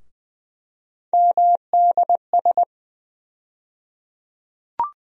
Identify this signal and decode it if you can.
Morse code, heard as a keyed tone, about 20 wpm, 700 Hz. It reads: MDS
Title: USA Call Sign Postfixes - 3 Letters 20wpm